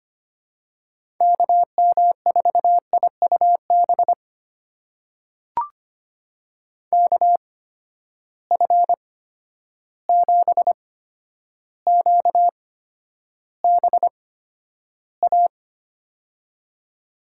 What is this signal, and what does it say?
Morse code, heard as a keyed tone, about 25 wpm, spaced as for 11 wpm, 700 Hz. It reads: KM4IUB KF7QBA
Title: USA Call Signs - 6 Characters - 8x Character Spacing - 25wpm